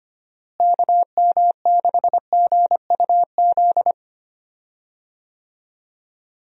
Morse code, heard as a keyed tone, about 25 wpm, 700 Hz. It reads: KM6GUZ
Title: USA Call Signs - 6 Characters - Repeated 2x - 25wpm